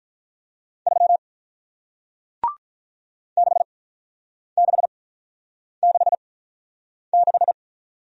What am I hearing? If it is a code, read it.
Morse code, heard as a keyed tone, about 46 wpm, spaced as for 9 wpm, 700 Hz. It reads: 3 6666